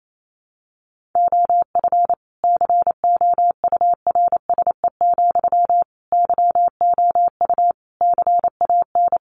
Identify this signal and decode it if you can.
Morse code, heard as a keyed tone, about 28 wpm, 700 Hz. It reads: OF COURSE, YOU CAN
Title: Sentences from Top 1500 Words 28wpm